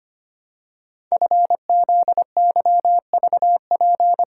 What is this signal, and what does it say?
Morse code, heard as a keyed tone, about 25 wpm, 700 Hz. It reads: FZYVP
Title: Code Groups - 5 Letter:Numbers - 25wpm